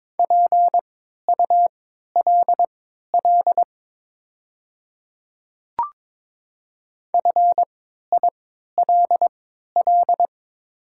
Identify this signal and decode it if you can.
Morse code, heard as a keyed tone, about 22 wpm, spaced as for 12 wpm, 700 Hz. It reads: PULL FILL
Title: 3x Character Spacing - Top 500 Words - 22wpm